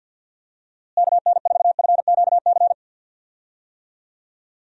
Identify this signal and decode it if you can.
Morse code, heard as a keyed tone, about 50 wpm, 700 Hz. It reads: KN4FQC
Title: USA Call Signs - 6 Characters - Repeated 2x - 50wpm